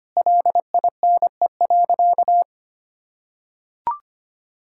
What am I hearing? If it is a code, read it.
Morse code, heard as a keyed tone, about 25 wpm, 700 Hz. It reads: LINE.
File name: practice / 25wpm / Sentences with 4-Word Phrases - 25wpm